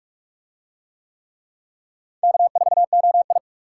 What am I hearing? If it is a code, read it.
Morse code, heard as a keyed tone, about 42 wpm, 700 Hz. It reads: K4OI